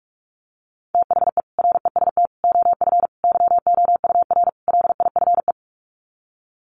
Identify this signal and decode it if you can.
Morse code, heard as a keyed tone, about 44 wpm, 700 Hz. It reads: THE REST OF YOUR LIFE
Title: Sets of 5 Words 45wpm